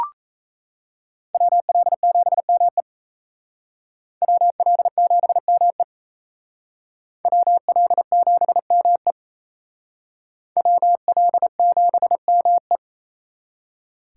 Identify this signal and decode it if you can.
Morse code, heard as a keyed tone, about 34 wpm, 700 Hz. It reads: WL7ME WL7ME WL7ME WL7ME
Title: Speed Racing - USA 2x2 Call Signs 28wpm